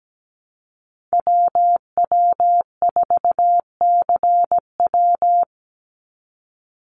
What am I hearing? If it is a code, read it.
Morse code, heard as a keyed tone, about 17 wpm, 700 Hz. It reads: WW4CW
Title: USA Call Signs 17wpm